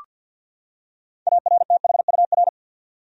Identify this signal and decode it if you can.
Morse code, heard as a keyed tone, about 47 wpm, 700 Hz. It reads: ARTHUR
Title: USA Common Names - 50wpm